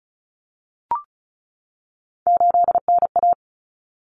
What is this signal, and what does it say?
Morse code, heard as a keyed tone, about 34 wpm, 700 Hz. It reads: 8NA
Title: Common Cut Numbers - 3-Digit 35wpm